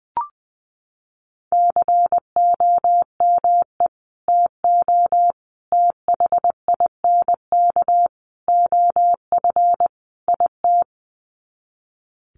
Morse code, heard as a keyed tone, about 20 wpm, 700 Hz. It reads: COME TO THINK OF IT
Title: Sets of 5 Words Encore 20wpm